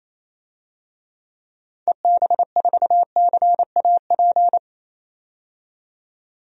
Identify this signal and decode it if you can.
Morse code, heard as a keyed tone, about 28 wpm, 700 Hz. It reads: EB4CAP
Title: DXCC-100 Call Signs - 6 Characters - 28wpm